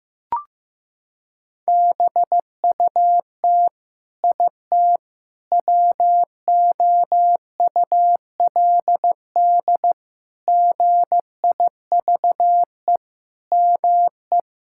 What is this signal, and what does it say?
Morse code, heard as a keyed tone, about 15 wpm, 700 Hz. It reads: BUT IT WOULD GIVE ME
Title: Sentences from Top 1600 Words 15wpm